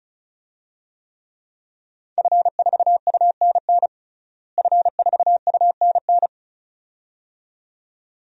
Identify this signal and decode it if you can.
Morse code, heard as a keyed tone, about 36 wpm, 700 Hz. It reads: F4UNN F4UNN
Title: DXCC-100 Call Signs - 5 Characters - Repeated 2x - 35wpm